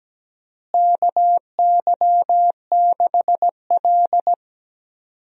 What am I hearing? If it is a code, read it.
Morse code, heard as a keyed tone, about 17 wpm, 700 Hz. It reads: KY6L
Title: USA Call Signs - 4 Characters - Repeated 2x - 17wpm